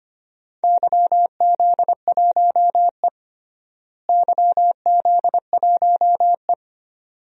Morse code, heard as a keyed tone, about 25 wpm, 700 Hz. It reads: YZ1E YZ1E
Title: DXCC-100 Call Signs - 4 Characters - Repeated 2x - 25wpm